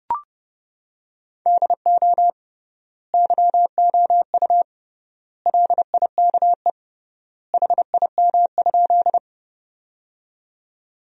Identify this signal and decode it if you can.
Morse code, heard as a keyed tone, about 30 wpm, 700 Hz. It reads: DO YOU LIKE HIM?
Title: 3x Word-Spacing - Sentences from Top 100 Words - 30wpm